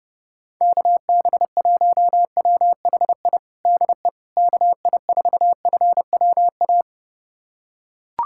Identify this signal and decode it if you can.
Morse code, heard as a keyed tone, about 32 wpm, 700 Hz. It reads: KB1WHI DE KI4FWA
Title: USA Callsign DE Callsign - 6 Characters - 30wpm